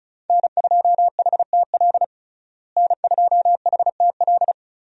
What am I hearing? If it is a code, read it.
Morse code, heard as a keyed tone, about 34 wpm, 700 Hz. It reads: N2HTL N2HTL
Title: USA Call Signs - 5 Characters - Repeated 3x - 35wpm